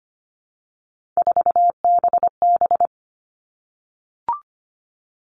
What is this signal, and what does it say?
Morse code, heard as a keyed tone, about 25 wpm, 700 Hz. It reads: 4BB
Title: All Cut Numbers - 3-Digit 25wpm